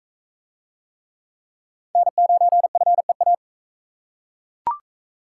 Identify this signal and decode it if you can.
Morse code, heard as a keyed tone, about 38 wpm, 700 Hz. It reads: N9FEA